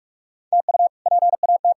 Morse code, heard as a keyed tone, about 46 wpm, 700 Hz. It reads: TU PAT
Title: K1USN SST QSO Practice - 45wpm